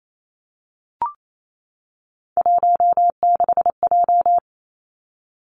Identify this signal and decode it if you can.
Morse code, heard as a keyed tone, about 28 wpm, 700 Hz. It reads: 16J
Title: Three Challenging Letter-Number Combinations 28wpm